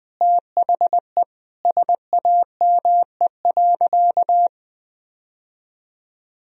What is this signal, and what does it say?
Morse code, heard as a keyed tone, about 20 wpm, 700 Hz. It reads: THE SAME.
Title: Sentences from Top 1000 Words - Review Part 1 - 20wpm